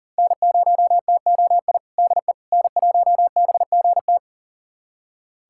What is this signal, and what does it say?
Morse code, heard as a keyed tone, about 37 wpm, 700 Hz. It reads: N0TOI DE N1BGT
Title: USA Callsign DE Callsign - 5 Characters - 40wpm